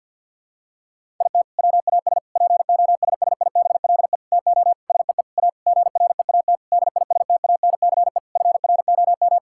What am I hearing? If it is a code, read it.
Morse code, heard as a keyed tone, about 51 wpm, 700 Hz. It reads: IT WAS POSSIBLE TO SEE A GREAT DISTANCE FROM
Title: Sentences from Top 1600 Words 50wpm